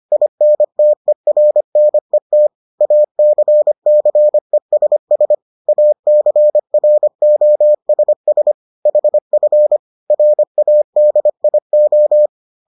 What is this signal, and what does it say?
Morse code, heard as a keyed tone, about 25 wpm, 600 Hz. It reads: INTERNET ACCESS ACROSS HF RADIO